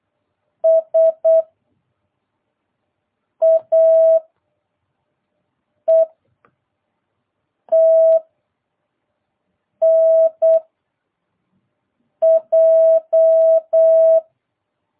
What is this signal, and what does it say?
Morse code, heard as a keyed tone, about 8 wpm, 650 Hz. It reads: S A E T N J